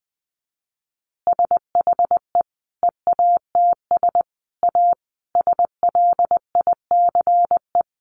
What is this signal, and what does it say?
Morse code, heard as a keyed tone, about 20 wpm, 700 Hz. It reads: SHE EATS A SLICE
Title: Additional Easy Phrases 20wpm